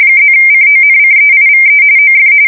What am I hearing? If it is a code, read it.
RTTY, about 45.45 baud. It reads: +CONCEIVE OF
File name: fsk170 / usnavy4